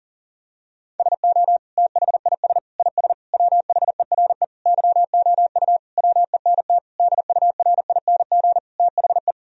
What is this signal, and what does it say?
Morse code, heard as a keyed tone, about 37 wpm, 700 Hz. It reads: SO THIS IS WHERE YOU WENT DURING THE
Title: Sentences from Top 1000 Words - Review Part 2 - 40wpm